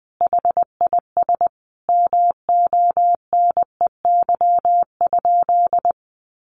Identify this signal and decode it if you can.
Morse code, heard as a keyed tone, about 20 wpm, 700 Hz. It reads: HIS MONEY?
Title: Sentences from Top 400 Words 20wpm